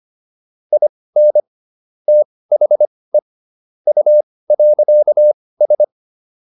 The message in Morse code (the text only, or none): IN THE U.S